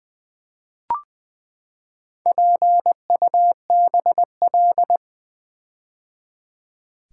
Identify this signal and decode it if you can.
Morse code, heard as a keyed tone, about 20 wpm, 700 Hz. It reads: PUBL
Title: Build A Word - 4 Letters At A Time - 11-Letter Words - 20wpm